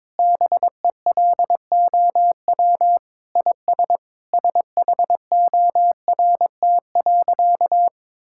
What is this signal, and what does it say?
Morse code, heard as a keyed tone, about 22 wpm, 700 Hz. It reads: BELOW IS SHORT.
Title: Sentences from Top 1000 Words - Review Part 5 - 22wpm